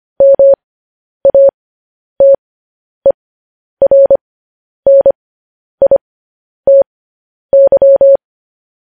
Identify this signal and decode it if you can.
Morse code, heard as a keyed tone, about 25 wpm, spaced as for 10 wpm, 550 Hz. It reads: MATERNITY
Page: news-headlines-in-morse-code-at-10-wpm